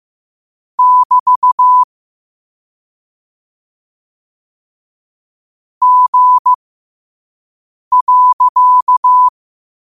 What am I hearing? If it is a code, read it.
Morse code, heard as a keyed tone, about 15 wpm, spaced as for 5 wpm, 1000 Hz. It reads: <BT> G.
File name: qotdfiles / 5wpm